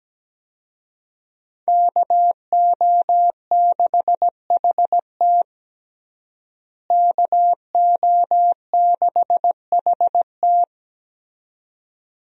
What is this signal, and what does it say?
Morse code, heard as a keyed tone, about 17 wpm, 700 Hz. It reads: KO6HT KO6HT
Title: USA Call Signs - 5 Characters - Repeated 2x - 17wpm